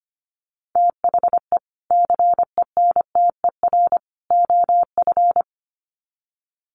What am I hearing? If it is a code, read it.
Morse code, heard as a keyed tone, about 25 wpm, 700 Hz. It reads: THE CENTER OF
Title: Sets of 3 Words 25wpm